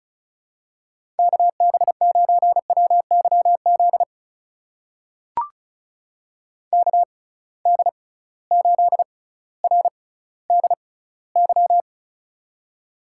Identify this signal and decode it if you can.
Morse code, heard as a keyed tone, about 36 wpm, spaced as for 18 wpm, 700 Hz. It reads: KB9WYZ KD8RDY
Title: USA Call Signs - 6 Characters - 6x Character Spacing - 35wpm